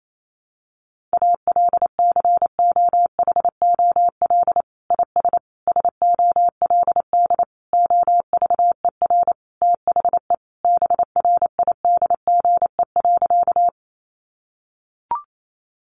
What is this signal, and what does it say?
Morse code, heard as a keyed tone, about 28 wpm, 700 Hz. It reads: ALCOHOL IS SOLD OVER THE BRIDGE.
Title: Sentences from Top 1400 Words 28wpm